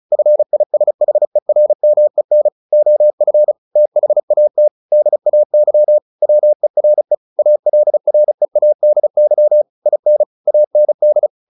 Morse code, heard as a keyed tone, about 34 wpm, 600 Hz. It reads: FISHERMEN OF THAT DAY WERE ALREADY IN AND
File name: the old man and the sea / the old man and the sea 1 35wpm